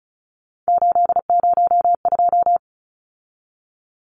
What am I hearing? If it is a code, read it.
Morse code, heard as a keyed tone, about 35 wpm, 700 Hz. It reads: 802
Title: Numbers - 3-Digit 35wpm